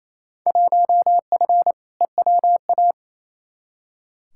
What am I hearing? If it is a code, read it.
Morse code, heard as a keyed tone, about 28 wpm, 700 Hz. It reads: 1F EWA
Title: Field Day Exchanges Encore - 28wpm